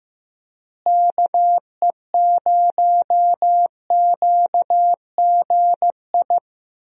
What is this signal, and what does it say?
Morse code, heard as a keyed tone, about 15 wpm, 700 Hz. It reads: KE0QGI
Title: USA Call Signs - 6 Characters - Repeated 2x - 15wpm